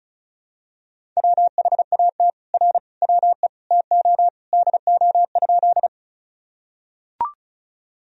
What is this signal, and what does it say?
Morse code, heard as a keyed tone, about 34 wpm, 700 Hz. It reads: WHAT R WE TO DO?
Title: Sentences from Top 100 Words and Abbreviations 35wpm